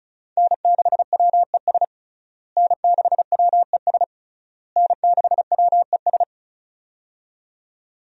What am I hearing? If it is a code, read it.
Morse code, heard as a keyed tone, about 34 wpm, 700 Hz. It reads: N6WES N6WES N6WES
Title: USA Call Signs - 5 Characters - Repeated 3x - 35wpm